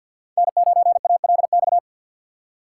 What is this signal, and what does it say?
Morse code, heard as a keyed tone, about 52 wpm, 700 Hz. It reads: N9ALX
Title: USA Call Signs - 5 Characters - 3x Character Spacing - 50wpm